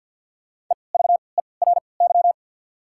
Morse code, heard as a keyed tone, about 54 wpm, spaced as for 29 wpm, 700 Hz. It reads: EVERY
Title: 3x Character Spacing - Top 200 Words - 50wpm